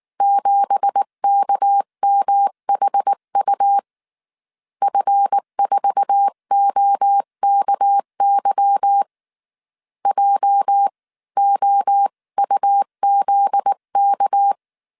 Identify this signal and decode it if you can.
Morse code, heard as a keyed tone, about 19 wpm, 800 Hz. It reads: 7KMHU F4OKY JOUZK
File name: CW / DET / 1 9 1500C S C 18wpm 2539